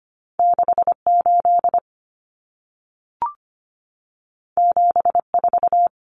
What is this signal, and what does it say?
Morse code, heard as a keyed tone, about 25 wpm, 700 Hz. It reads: 68 74
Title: Numbers - 2-Digit 25wpm